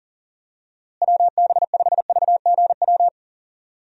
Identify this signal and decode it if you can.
Morse code, heard as a keyed tone, about 37 wpm, 700 Hz. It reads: WB5VGW
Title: USA Call Signs - 6 Characters - Repeated 3x - 40wpm